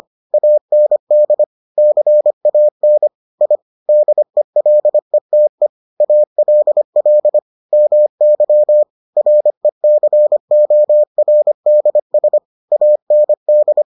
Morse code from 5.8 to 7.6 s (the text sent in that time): ALL